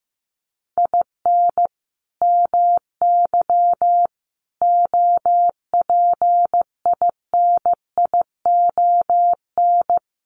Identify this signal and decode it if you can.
Morse code, heard as a keyed tone, about 14 wpm, 700 Hz. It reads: IN MY OPINION
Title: Sentences from Top 1200 Words 15wpm